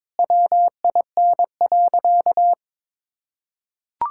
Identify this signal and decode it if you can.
Morse code, heard as a keyed tone, about 21 wpm, 700 Hz. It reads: WIN.
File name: practice / 22wpm / Sentences with 3-Word Phrases - 22wpm